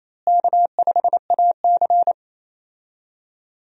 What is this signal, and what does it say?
Morse code, heard as a keyed tone, about 28 wpm, 700 Hz. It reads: K5AC
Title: USA Call Signs - 4 Characters - Repeated 2x - 28wpm